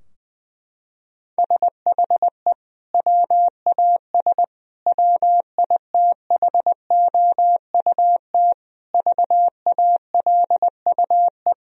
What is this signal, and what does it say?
Morse code, heard as a keyed tone, about 20 wpm, 700 Hz. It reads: SHE WAS WITHOUT VALUE
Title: Sentences from Top 1000 Words - Review Part 3 - 20wpm